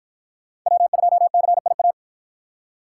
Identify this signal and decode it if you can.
Morse code, heard as a keyed tone, about 49 wpm, 700 Hz. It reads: W2CIA